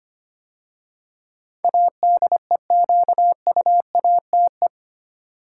Morse code, heard as a keyed tone, about 25 wpm, 700 Hz. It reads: ADEQUATE
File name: 8 Letter Words - First 3 Letters Given - 25wpm